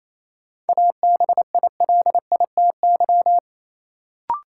Morse code, heard as a keyed tone, about 28 wpm, 700 Hz. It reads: ABILITY
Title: Sentences from Top 1000 Words - Review Part 3 - New Word Drill - 28wpm